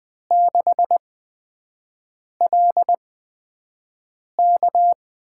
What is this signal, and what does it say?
Morse code, heard as a keyed tone, about 20 wpm, spaced as for 5 wpm, 700 Hz. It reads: 6LK
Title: USA Call Signs - 5 Characters - 8x Character Spacing - 20wpm